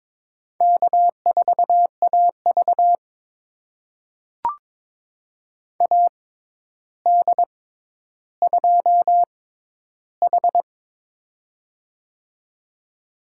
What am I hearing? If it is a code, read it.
Morse code, heard as a keyed tone, about 22 wpm, spaced as for 11 wpm, 700 Hz. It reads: K4AV AD2H
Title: USA Call Signs - 4 Characters - 6x Character Spacing - 22wpm